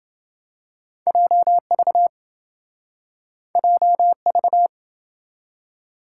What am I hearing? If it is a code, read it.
Morse code, heard as a keyed tone, about 28 wpm, 700 Hz. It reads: JV JV